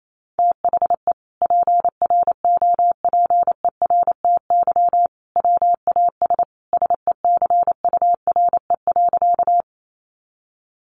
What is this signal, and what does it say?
Morse code, heard as a keyed tone, about 28 wpm, 700 Hz. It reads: THE PROPERTY WAS SECURE.
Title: Sentences from Top 1200 Words 28wpm